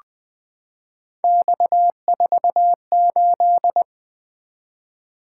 Morse code, heard as a keyed tone, about 20 wpm, 700 Hz. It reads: X48
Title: Three Letter-Number Combinations - 20wpm